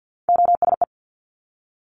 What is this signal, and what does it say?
Morse code, heard as a keyed tone, about 50 wpm, 700 Hz. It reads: OSE
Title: Top Three Letter Combinations in Words 50wpm